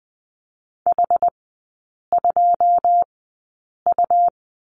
Morse code, heard as a keyed tone, about 20 wpm, spaced as for 8 wpm, 700 Hz. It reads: H2U
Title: 100 Lesson 17e - Letter u - Three Characters 20wpm